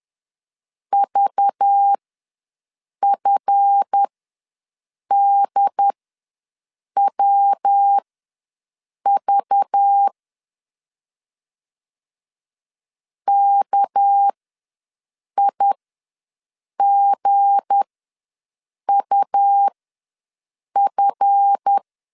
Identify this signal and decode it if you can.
Morse code, heard as a keyed tone, about 10 wpm, spaced as for 6 wpm, 800 Hz. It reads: VFDWV KIGUF
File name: LEZIONI / 7 GF9Z / 7 GF ws 10wpm 3918